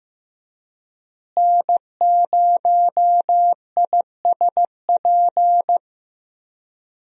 Morse code, heard as a keyed tone, about 15 wpm, 700 Hz. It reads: N0ISP
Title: USA Call Signs - 5 Characters - Repeated 3x - 15wpm